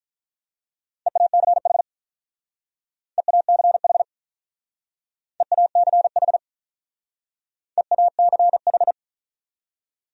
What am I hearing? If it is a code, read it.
Morse code, heard as a keyed tone, about 46 wpm, 700 Hz. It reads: EACH EACH EACH EACH